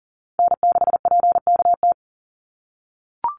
Morse code, heard as a keyed tone, about 40 wpm, 700 Hz. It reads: N6PKT
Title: NA SOTA Chaser Callsigns 40wpm